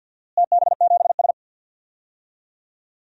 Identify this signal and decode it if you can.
Morse code, heard as a keyed tone, about 53 wpm, 700 Hz. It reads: TB7S
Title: DXCC-100 Call Signs - 4 Characters - Repeated 3x - 50wpm